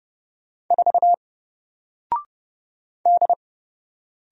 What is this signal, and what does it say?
Morse code, heard as a keyed tone, about 31 wpm, 700 Hz. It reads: 4 D
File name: Single-Letter-Number-30wpm